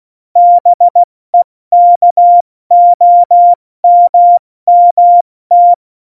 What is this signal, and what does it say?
Morse code, heard as a keyed tone, about 16 wpm, 700 Hz. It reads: BEKOMMT
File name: Morsekurs / Witze / Witz 14. August 2018, 70 F80